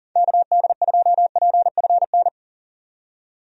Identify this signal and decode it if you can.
Morse code, heard as a keyed tone, about 40 wpm, 700 Hz. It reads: KD2PFN
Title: USA Call Signs - 6 Characters - 40wpm